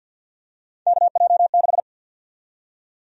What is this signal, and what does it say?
Morse code, heard as a keyed tone, about 53 wpm, 700 Hz. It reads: KJ6
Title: USA Call Sign Prefixes - 50wpm